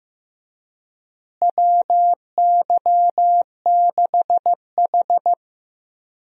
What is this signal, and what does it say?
Morse code, heard as a keyed tone, about 15 wpm, 700 Hz. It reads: WY6H
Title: USA Call Signs - 4 Characters - Repeated 3x - 15wpm